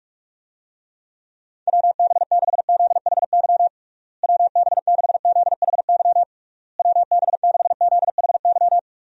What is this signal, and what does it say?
Morse code, heard as a keyed tone, about 41 wpm, 700 Hz. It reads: WB6ZHY WB6ZHY WB6ZHY
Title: USA Call Signs - 6 Characters - Repeated 3x - 45wpm